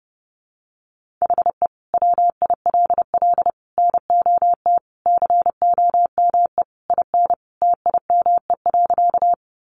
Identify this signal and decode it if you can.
Morse code, heard as a keyed tone, about 30 wpm, 700 Hz. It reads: HE WILL NOT COME IN TIME.
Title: Sentences from Top 600 Words 30wpm